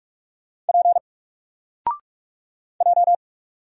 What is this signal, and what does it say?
Morse code, heard as a keyed tone, about 46 wpm, 700 Hz. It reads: P J
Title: Single Letter-Number Alt 45wpm